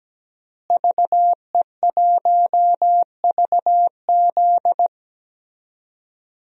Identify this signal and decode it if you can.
Morse code, heard as a keyed tone, about 17 wpm, 700 Hz. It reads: VE1VZ
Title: DXCC-100 Call Signs - 5 Characters - Repeated 3x - 17wpm